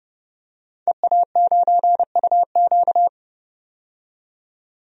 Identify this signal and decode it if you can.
Morse code, heard as a keyed tone, about 30 wpm, 700 Hz. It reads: EA9UQ
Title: DXCC-100 Call Signs - 5 Characters - 30wpm